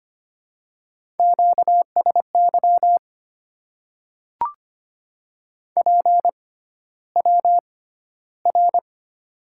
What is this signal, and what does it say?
Morse code, heard as a keyed tone, about 25 wpm, spaced as for 13 wpm, 700 Hz. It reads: QSY PWR
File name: Top 90 QSO Elements - 6x Character Spacing - 25wpm